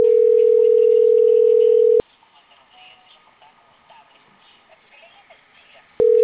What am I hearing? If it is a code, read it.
Ringback tone (North America), 440 + 480 Hz.